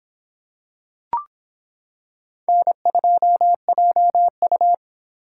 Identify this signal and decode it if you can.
Morse code, heard as a keyed tone, about 26 wpm, 700 Hz. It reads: N2JU